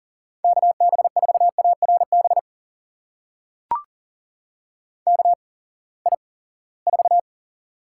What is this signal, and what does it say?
Morse code, heard as a keyed tone, about 40 wpm, spaced as for 23 wpm, 700 Hz. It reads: KB4ARB KI4
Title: USA Call Signs - 8x Character Spacing - 40wpm